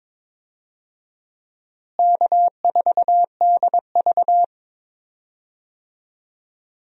Morse code, heard as a keyed tone, about 22 wpm, 700 Hz. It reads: K4DV
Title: USA Call Signs - 4 Characters - Repeated 2x - 22wpm